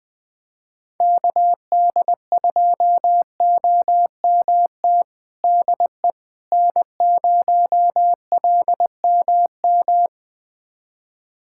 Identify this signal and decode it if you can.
Morse code, heard as a keyed tone, about 21 wpm, 700 Hz. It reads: KD2OMT DE N0LMM
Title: USA Callsign DE Callsign 20wpm